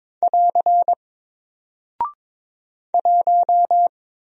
Morse code, heard as a keyed tone, about 22 wpm, 700 Hz. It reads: <AR> 1
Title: Single Character (Letter:Number:Punctuation:Prosign) 22wpm